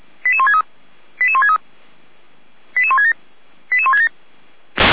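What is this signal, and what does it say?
Five-tone selective call (ZVEI), heard as quick runs of five tones, tones of about 70 ms. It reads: 88163 88163 88167 88167
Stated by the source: Received on SDR for feuerwehr-orlamuende.org fire brigade.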